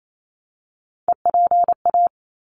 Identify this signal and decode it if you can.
Morse code, heard as a keyed tone, about 28 wpm, 700 Hz. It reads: EPA
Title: ARRL Sections 28wpm